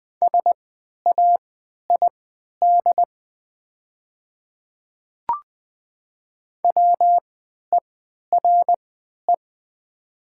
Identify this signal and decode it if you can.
Morse code, heard as a keyed tone, about 20 wpm, spaced as for 11 wpm, 700 Hz. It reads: SAID WERE
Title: 3x Character Spacing - Top 100 Words - 20wpm